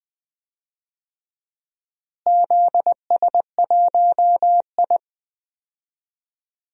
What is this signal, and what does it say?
Morse code, heard as a keyed tone, about 20 wpm, 700 Hz. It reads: ZS1I
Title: DXCC-100 Call Signs - 4 Characters - Repeated 3x - 20wpm